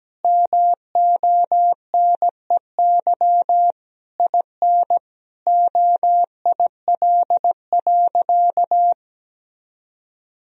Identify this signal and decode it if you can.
Morse code, heard as a keyed tone, about 17 wpm, 700 Hz. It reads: MONEY IN OIL.
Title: Sentences from Top 1000 Words - Review Part 4 - 17wpm